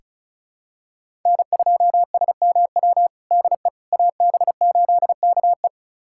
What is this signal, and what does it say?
Morse code, heard as a keyed tone, about 36 wpm, 700 Hz. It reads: N2SMW DE AB8KE
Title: USA Callsign DE Callsign - 5 Characters - 35wpm